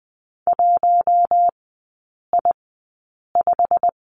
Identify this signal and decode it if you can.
Morse code, heard as a keyed tone, about 20 wpm, spaced as for 8 wpm, 700 Hz. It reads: 1I5
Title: 107 Lesson 18e - Letter m - Three Characters 20wpm